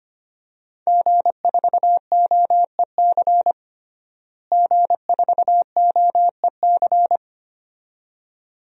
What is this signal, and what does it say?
Morse code, heard as a keyed tone, about 25 wpm, 700 Hz. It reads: G4OEC G4OEC
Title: DXCC-100 Call Signs - 5 Characters - Repeated 2x - 25wpm